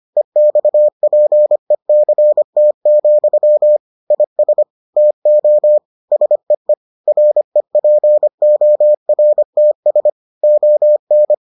Morse code, heard as a keyed tone, about 25 wpm, 600 Hz. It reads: EXPECT, IS TO SEE REPORTS ON